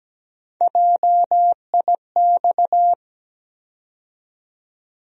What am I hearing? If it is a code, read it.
Morse code, heard as a keyed tone, about 17 wpm, 700 Hz. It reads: JIX
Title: USA Call Sign Postfixes - 3 Letters 17wpm